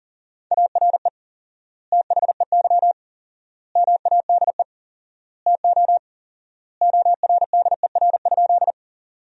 Morse code, heard as a keyed tone, about 37 wpm, 700 Hz. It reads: ARE THEY MADE TO ORDER?
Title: 4x Word-Spacing - Sentences from Top 400 Words - 40wpm